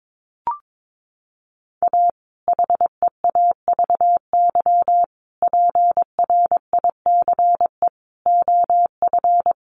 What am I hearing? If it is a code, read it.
Morse code, heard as a keyed tone, about 22 wpm, 700 Hz. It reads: A HEAVY PRICE OF
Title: Sentences from Top 1100 Words 22wpm